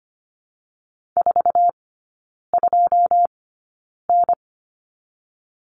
Morse code, heard as a keyed tone, about 25 wpm, spaced as for 9 wpm, 690 Hz. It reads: 42N
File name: Lesson 14e - Number 2 - Three Characters 25wpm